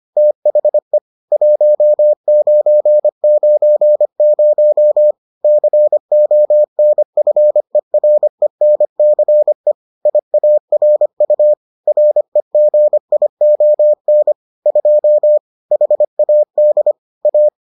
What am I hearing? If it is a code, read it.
Morse code, heard as a keyed tone, about 25 wpm, 600 Hz. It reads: THE 1990 CONFERENCE IARU REGION 2 HAD A